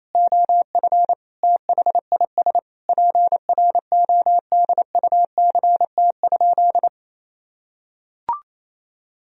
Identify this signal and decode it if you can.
Morse code, heard as a keyed tone, about 28 wpm, 700 Hz. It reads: OF THIS PRODUCT?
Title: Sentences from Top 1000 Words - Review Part 3 - 28wpm